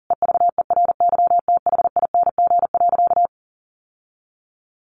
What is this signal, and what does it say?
Morse code, heard as a keyed tone, about 40 wpm, 700 Hz. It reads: EVERYTHING.
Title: Sentences from Top 1100 Words 40wpm